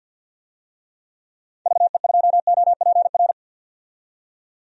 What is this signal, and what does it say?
Morse code, heard as a keyed tone, about 51 wpm, 700 Hz. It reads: VE2OPR
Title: DXCC-100 Call Signs - 6 Characters - 50wpm